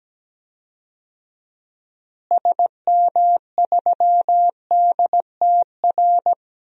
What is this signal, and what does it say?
Morse code, heard as a keyed tone, about 17 wpm, 700 Hz. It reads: SM3DTR
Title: DXCC-100 Call Signs - 6 Characters - Repeated 3x - 17wpm